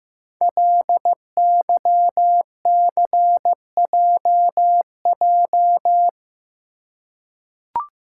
Morse code, heard as a keyed tone, about 15 wpm, 700 Hz. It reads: LYCJJ
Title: Code Groups - 5 Letter:Numbers - 15wpm